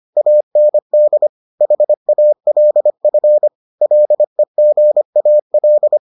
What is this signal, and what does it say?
Morse code, heard as a keyed tone, about 25 wpm, 600 Hz. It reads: AND HALF LEGAL